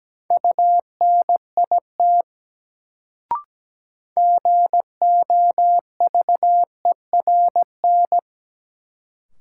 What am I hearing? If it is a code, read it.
Morse code, heard as a keyed tone, about 17 wpm, 700 Hz. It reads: UNIT GOVERN
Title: Sentences from Top 1000 Words - Review Part 2 - New Word Drill - 17wpm